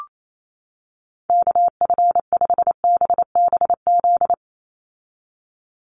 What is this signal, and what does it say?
Morse code, heard as a keyed tone, about 28 wpm, 700 Hz. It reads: KF5BBZ
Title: USA Call Signs 28wpm